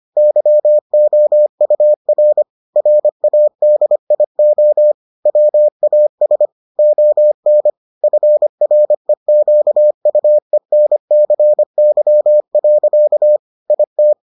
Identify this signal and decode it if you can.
Morse code, heard as a keyed tone, about 25 wpm, 600 Hz. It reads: YOUR RADIO WAS ON FREQUENCY. IT